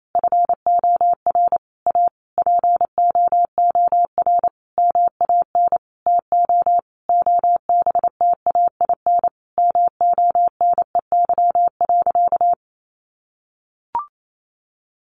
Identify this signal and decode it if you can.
Morse code, heard as a keyed tone, about 28 wpm, 700 Hz. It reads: FOR A POOR MAN TO OBTAIN MONEY.
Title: Sentences from Top 1300 Words 28wpm